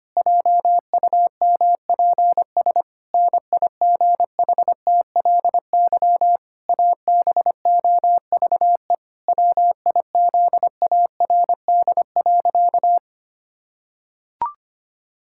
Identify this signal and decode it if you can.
Morse code, heard as a keyed tone, about 25 wpm, 700 Hz. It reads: JUMPS NIGHTLY ABOVE WIZARD.
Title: Pangram Challenge v2 - 25wpm